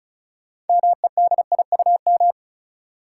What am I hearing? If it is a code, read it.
Morse code, heard as a keyed tone, about 36 wpm, 700 Hz. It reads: MEDIUM
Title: Sentences from Top 900 Words - New Word Drill - 35wpm